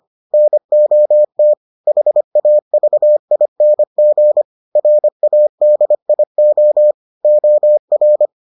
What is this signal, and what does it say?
Morse code, heard as a keyed tone, about 25 wpm, 600 Hz. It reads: NOT HAVING RADIO OR